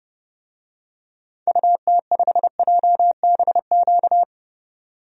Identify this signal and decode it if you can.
Morse code, heard as a keyed tone, about 30 wpm, 700 Hz. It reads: UT5JBQ
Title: DXCC-100 Call Signs - 6 Characters - 30wpm